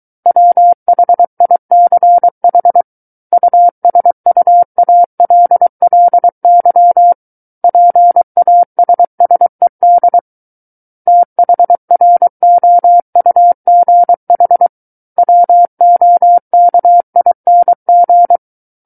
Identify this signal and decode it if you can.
Morse code, heard as a keyed tone, about 23 wpm, 700 Hz. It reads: WHICH USUALLY PASSED THROUGH WOKING
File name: Book One / 14-Chapter 14 - 23 WPM